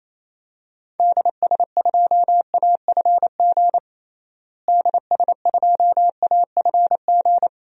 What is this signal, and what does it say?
Morse code, heard as a keyed tone, about 28 wpm, 700 Hz. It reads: DS2AFG DS2AFG
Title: DXCC-100 Call Signs - 6 Characters - Repeated 2x - 28wpm